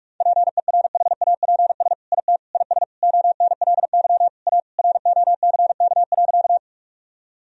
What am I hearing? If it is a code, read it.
Morse code, heard as a keyed tone, about 46 wpm, 700 Hz. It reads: PERHAPS IT IS ONLY A ROCK.
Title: Sentences from Top 1000 Words - Review Part 2 - 45wpm